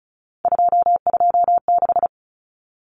Morse code, heard as a keyed tone, about 34 wpm, 700 Hz. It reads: 226
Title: Numbers - 3-Digit 35wpm